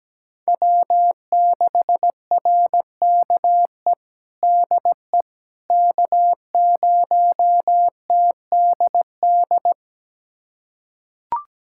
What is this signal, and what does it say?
Morse code, heard as a keyed tone, about 17 wpm, 700 Hz. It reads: W6RKE DE K0TDD
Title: USA Callsign DE Callsign - 5 Characters - 17wpm